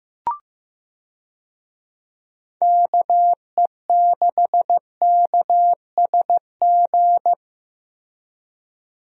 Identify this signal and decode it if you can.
Morse code, heard as a keyed tone, about 14 wpm, 700 Hz. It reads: KE6KSG
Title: USA Call Signs - 6 Characters - Repeated 2x - 15wpm